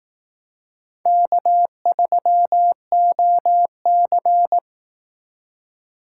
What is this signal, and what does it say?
Morse code, heard as a keyed tone, about 18 wpm, 700 Hz. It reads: K3OC
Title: Speed Racing - USA 1x2 Call Signs 15wpm